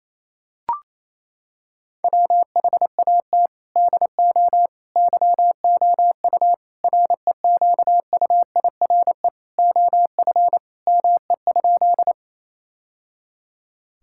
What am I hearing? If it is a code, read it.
Morse code, heard as a keyed tone, about 28 wpm, 700 Hz. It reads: WHAT DO YOU REQUIRE OF ME?
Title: Sentences from Top 1000 Words - Review Part 6 - 28wpm